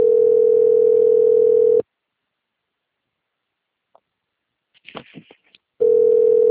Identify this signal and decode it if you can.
Ringback tone (North America), 440 + 480 Hz.